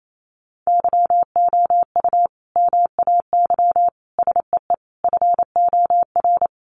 Morse code, heard as a keyed tone, about 28 wpm, 700 Hz. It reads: YOU MAY SEE FOR
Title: Sentences from Top 600 Words 28wpm